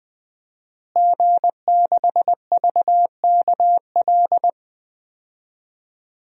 Morse code, heard as a keyed tone, about 21 wpm, 700 Hz. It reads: G6VKL
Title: DXCC-100 Call Signs - 5 Characters - Repeated 3x - 20wpm